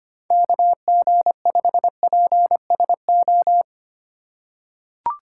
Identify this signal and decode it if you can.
Morse code, heard as a keyed tone, about 25 wpm, 700 Hz. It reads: KG5PSO
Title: USA Call Signs - 6 Characters - 3x Character Spacing - 25wpm